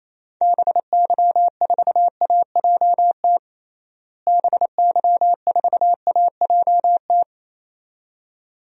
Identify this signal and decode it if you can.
Morse code, heard as a keyed tone, about 27 wpm, 700 Hz. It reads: BY4AJT BY4AJT
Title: DXCC-100 Call Signs - 6 Characters - Repeated 2x - 28wpm